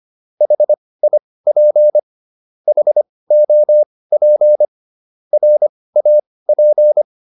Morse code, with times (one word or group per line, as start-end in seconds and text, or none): 0.4-2.0 HIP
2.7-4.7 HOP
5.3-7.0 RAP